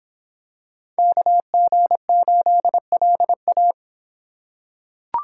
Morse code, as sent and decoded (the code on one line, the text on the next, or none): -.- --. ---.. .-.. .-
KG8LA